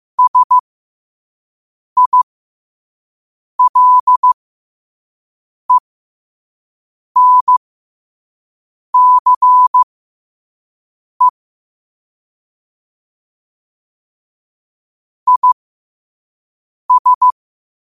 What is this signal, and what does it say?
Morse code, heard as a keyed tone, about 15 wpm, spaced as for 5 wpm, 1000 Hz. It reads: SILENCE IS
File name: qotdfiles / 5wpm